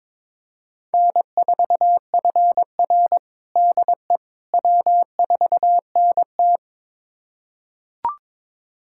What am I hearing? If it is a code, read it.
Morse code, heard as a keyed tone, about 22 wpm, 700 Hz. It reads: N4FR DE W4NT
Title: USA Callsign DE Callsign - 4 Characters - 22wpm